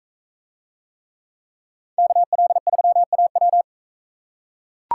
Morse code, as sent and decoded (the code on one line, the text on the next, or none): -.- .-.. ...-- .- .--
KL3AW